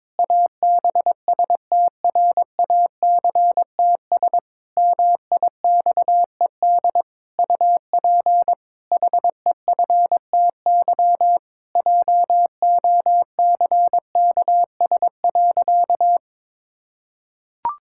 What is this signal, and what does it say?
Morse code, heard as a keyed tone, about 22 wpm, 700 Hz. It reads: ABSTRACTS MIXED UP HEFTY JOCKS.
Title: Pangram Challenge v2 - 22wpm